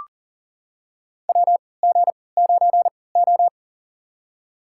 Morse code, heard as a keyed tone, about 39 wpm, spaced as for 22 wpm, 700 Hz. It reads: WG9O
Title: USA Call Signs - 3x Character Spacing - 40wpm